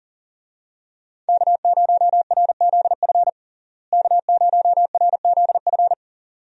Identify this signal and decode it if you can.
Morse code, heard as a keyed tone, about 40 wpm, 700 Hz. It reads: K0RZF K0RZF
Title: USA Call Signs - 5 Characters - Repeated 2x - 40wpm